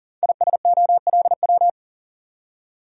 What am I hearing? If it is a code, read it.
Morse code, heard as a keyed tone, about 39 wpm, 700 Hz. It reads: ISOLW
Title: Code Groups - 5 Letter:Numbers - 40wpm